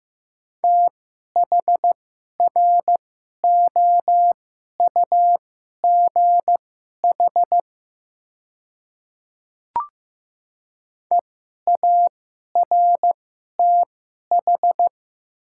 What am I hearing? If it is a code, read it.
Morse code, heard as a keyed tone, about 15 wpm, spaced as for 11 wpm, 700 Hz. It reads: THROUGH EARTH
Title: 2x Character Spacing - Top 300 Words - 15wpm